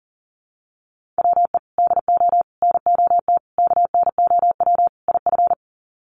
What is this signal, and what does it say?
Morse code, heard as a keyed tone, about 37 wpm, 700 Hz. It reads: WE DO NOT KNOW IF
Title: Sets of 5 Words Encore2 40wpm